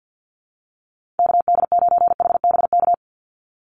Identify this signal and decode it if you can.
Morse code, heard as a keyed tone, about 50 wpm, 700 Hz. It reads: KD9HBK
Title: USA Call Signs 50wpm